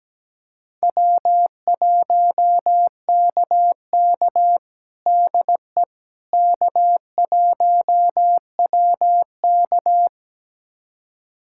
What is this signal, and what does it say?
Morse code, heard as a keyed tone, about 17 wpm, 700 Hz. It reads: W1KK DE K1WK
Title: USA Callsign DE Callsign - 4 Characters - 17wpm